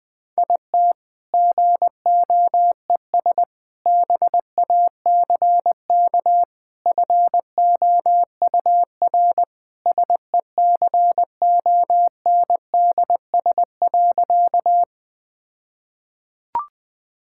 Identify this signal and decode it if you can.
Morse code, heard as a keyed tone, about 21 wpm, 700 Hz. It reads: IT GOES BACK FOUR SECONDS.
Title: All-Age Appropriate Jokes - 20wpm